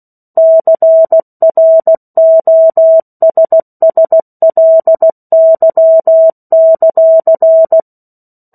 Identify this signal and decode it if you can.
Morse code, heard as a keyed tone, about 16 wpm, 650 Hz. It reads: CROSSLY;
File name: 03-Chapter-03 16 WPM